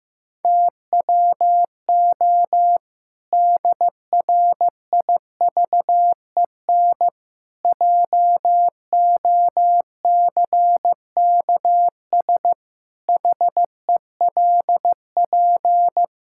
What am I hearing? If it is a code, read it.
Morse code, heard as a keyed tone, about 15 wpm, 700 Hz. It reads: TWO DRIVEN JOCKS HELP
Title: Pangram Challenge v2 - 15wpm